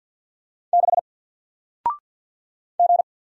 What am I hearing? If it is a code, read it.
Morse code, heard as a keyed tone, about 53 wpm, 700 Hz. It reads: 6 G
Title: Single Letter-Number 50wpm